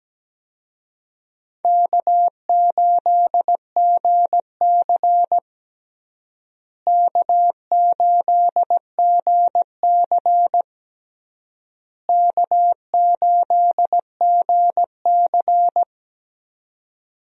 Morse code, heard as a keyed tone, about 17 wpm, 700 Hz. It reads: K8GC K8GC K8GC
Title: USA Call Signs - 4 Characters - Repeated 3x - 17wpm